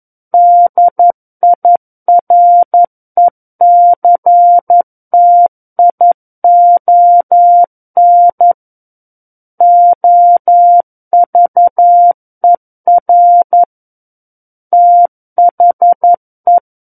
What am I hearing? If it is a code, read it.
Morse code, heard as a keyed tone, about 11 wpm, 700 Hz. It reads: DIRECTION OVER THE
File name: Book One / 02-Chapter 2 - 11 WPM